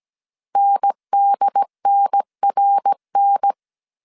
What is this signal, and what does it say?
Morse code, heard as a keyed tone, about 17 wpm, 800 Hz. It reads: NDNRN